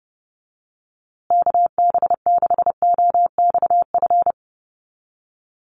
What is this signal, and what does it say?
Morse code, heard as a keyed tone, about 30 wpm, 700 Hz. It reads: KB6OXF
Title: USA Call Signs Repeated 2x 30wpm